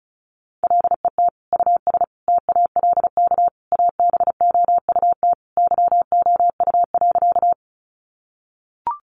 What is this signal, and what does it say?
Morse code, heard as a keyed tone, about 34 wpm, 700 Hz. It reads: LET US TALK ABOUT YOU.
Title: Sentences from Top 400 Words 35wpm